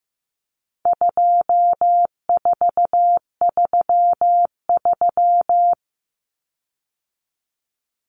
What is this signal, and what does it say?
Morse code, heard as a keyed tone, about 15 wpm, 700 Hz. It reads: 2433
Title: Common Cut Numbers - 4-Digit 15wpm